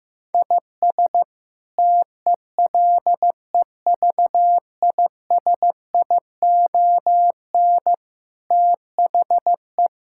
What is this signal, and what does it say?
Morse code, heard as a keyed tone, about 16 wpm, 700 Hz. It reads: IS TELEVISION THE
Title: Sentences from Top 1000 Words - Review Part 6 - 15wpm